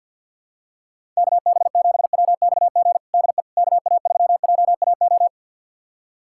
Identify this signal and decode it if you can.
Morse code, heard as a keyed tone, about 51 wpm, 700 Hz. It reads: KB7WXG DE KA3JAO
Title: USA Callsign DE Callsign 50wpm